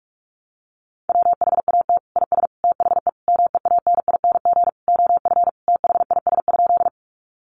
Morse code, heard as a keyed tone, about 44 wpm, 700 Hz. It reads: WHAT IS THE MEANING OF THIS?
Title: Sentences from Top 900 Words 45wpm